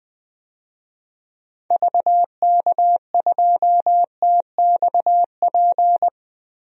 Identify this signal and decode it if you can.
Morse code, heard as a keyed tone, about 20 wpm, 700 Hz. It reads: VK2TXP
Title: DXCC-100 Call Signs - 6 Characters - 20wpm